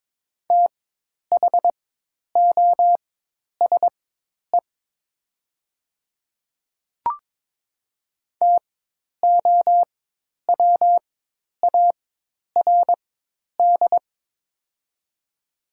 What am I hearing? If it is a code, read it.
Morse code, heard as a keyed tone, about 22 wpm, spaced as for 10 wpm, 700 Hz. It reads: THOSE TOWARD
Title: 4x Character Spacing - Top 500 Words - 22wpm